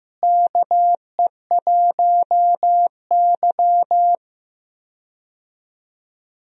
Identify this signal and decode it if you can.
Morse code, heard as a keyed tone, about 15 wpm, 700 Hz. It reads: KE1Y
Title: USA Call Signs - 4 Characters - Repeated 2x - 15wpm